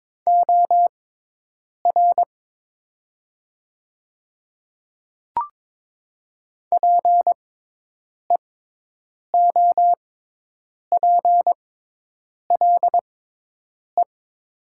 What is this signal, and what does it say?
Morse code, heard as a keyed tone, about 22 wpm, spaced as for 8 wpm, 700 Hz. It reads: OR PEOPLE
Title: 6x Character Spacing - Top 100 Words - 22wpm